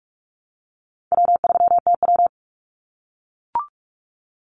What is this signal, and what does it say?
Morse code, heard as a keyed tone, about 44 wpm, 700 Hz. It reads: W3TW
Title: Easy Call Signs - 45wpm